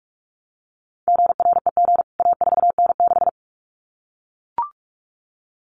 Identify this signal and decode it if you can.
Morse code, heard as a keyed tone, about 44 wpm, 700 Hz. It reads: GREG A4N6
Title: CWT Exchange 45wpm